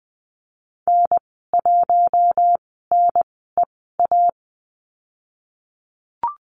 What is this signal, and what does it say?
Morse code, heard as a keyed tone, about 20 wpm, spaced as for 15 wpm, 700 Hz. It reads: N1NEA
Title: 055 Lesson 10g - Letter r - Call Signs 20wpm